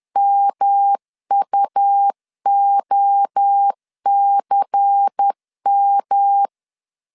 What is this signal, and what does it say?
Morse code, heard as a keyed tone, about 10 wpm, 800 Hz. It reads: MUOCM
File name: CW / DET / 1 6 1500C S B 10wpm 4137